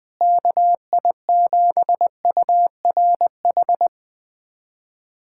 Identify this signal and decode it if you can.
Morse code, heard as a keyed tone, about 20 wpm, 700 Hz. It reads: KI7URH
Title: USA Callsign DE Callsign - 6 Characters - 20wpm